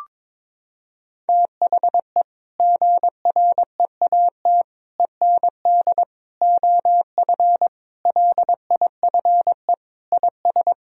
Morse code, heard as a keyed tone, about 22 wpm, 700 Hz. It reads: THE GREAT END OF LIFE IS
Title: Sentences from Top 1000 Words - Review Part 4 - 22wpm